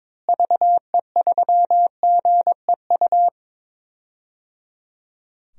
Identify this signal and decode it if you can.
Morse code, heard as a keyed tone, about 22 wpm, 700 Hz. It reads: VE3GEU
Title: DXCC-100 Call Signs - 6 Characters - Repeated 3x - 22wpm